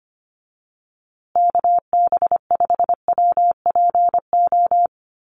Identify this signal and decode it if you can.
Morse code, heard as a keyed tone, about 25 wpm, 700 Hz. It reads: KB5WPO
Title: USA Call Signs Repeated 2x 25wpm